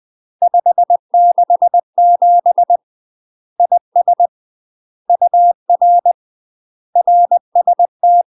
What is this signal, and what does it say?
Morse code, heard as a keyed tone, about 20 wpm, 700 Hz. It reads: 567 IS UR RST
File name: UK MASTER QSOs TRAINING / 20WPM / UK MASTER QSOs 0050